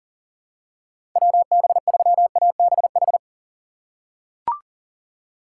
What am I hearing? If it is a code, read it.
Morse code, heard as a keyed tone, about 40 wpm, 700 Hz. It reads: WB3ABH
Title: USA Call Signs - 6 Characters - 4x Character Spacing - 40wpm